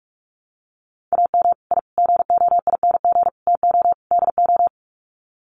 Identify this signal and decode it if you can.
Morse code, heard as a keyed tone, about 44 wpm, 700 Hz. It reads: AM I GOING TO DO
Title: Sets of 5 Words Encore 45wpm